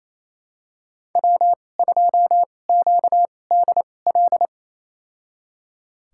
Morse code, heard as a keyed tone, about 28 wpm, spaced as for 20 wpm, 700 Hz. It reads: W2QDL
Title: USA Call Signs - 5 Characters - 2x Character Spacing - 28wpm